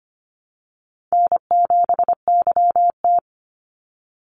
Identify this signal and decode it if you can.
Morse code, heard as a keyed tone, about 25 wpm, 700 Hz. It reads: N7YT
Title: Easy Call Signs - 25wpm